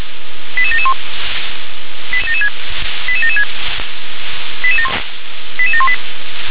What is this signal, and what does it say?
Five-tone selective call (ZVEI), heard as quick runs of five tones, tones of about 70 ms. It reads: 88771 88776 88718